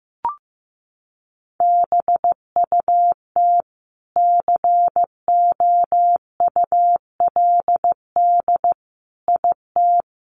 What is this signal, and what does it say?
Morse code, heard as a keyed tone, about 15 wpm, 700 Hz. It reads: BUT COULD IT
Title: Sentences from Top 1200 Words 15wpm